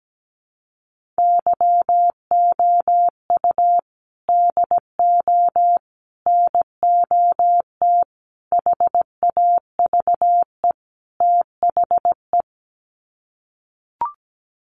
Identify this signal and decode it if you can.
Morse code, heard as a keyed tone, about 17 wpm, 700 Hz. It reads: YOU DO NOT HAVE THE
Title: Sets of 5 Words 17wpm